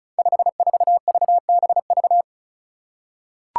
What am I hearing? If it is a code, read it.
Morse code, heard as a keyed tone, about 36 wpm, 700 Hz. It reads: 54VBV
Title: Five Challenging Letter-Number Combinations 35wpm